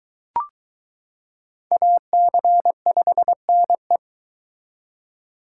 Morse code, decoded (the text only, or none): AC5NE